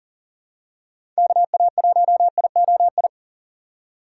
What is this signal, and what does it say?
Morse code, heard as a keyed tone, about 40 wpm, 700 Hz. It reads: KA1IOI
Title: USA Call Signs - 6 Characters - 4x Character Spacing - 40wpm